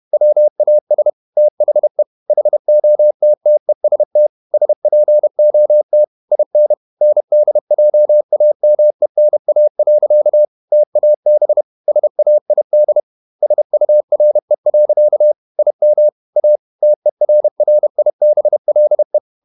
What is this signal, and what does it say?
Morse code, heard as a keyed tone, about 31 wpm, 600 Hz. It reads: WAS THE HOTTEST SPOT IN NDJAMENA. TAB SAID SURE. IM A TERRIBLE
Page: Never 31wpm | CW med Gnister